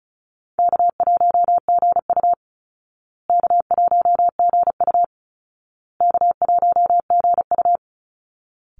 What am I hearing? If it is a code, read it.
Morse code, heard as a keyed tone, about 34 wpm, 700 Hz. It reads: K1GU K1GU K1GU
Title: Easy Call Signs Repeated 3x - 35wpm